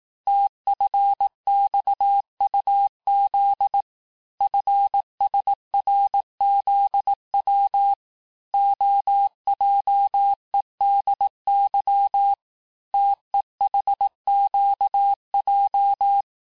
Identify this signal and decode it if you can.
Morse code, heard as a keyed tone, about 18 wpm, 800 Hz. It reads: TFXUZ FSRZW OJEDY TEHQJ
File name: cours-du-lundi files / cours-1588-texte-chiffre-vitesse-17MPM